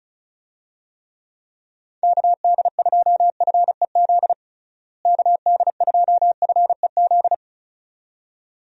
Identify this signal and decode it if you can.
Morse code, heard as a keyed tone, about 36 wpm, 700 Hz. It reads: KD2FEZ KD2FEZ
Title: USA Call Signs - 6 Characters - Repeated 2x - 35wpm